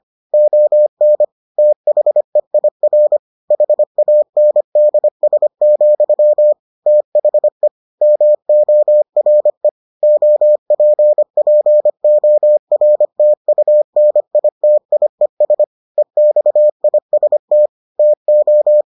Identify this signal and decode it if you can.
Morse code, heard as a keyed tone, about 25 wpm, 600 Hz. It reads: ON THEIR HANDS, THE MORE OPPORTUNITIES EXIST TO